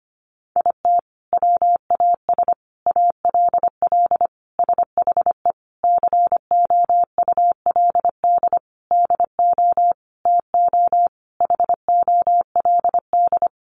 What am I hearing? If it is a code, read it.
Morse code, heard as a keyed tone, about 25 wpm, 700 Hz. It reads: IT WAS ALL SHE COULD DO TO HOLD
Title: Sentences from Top 400 Words 25wpm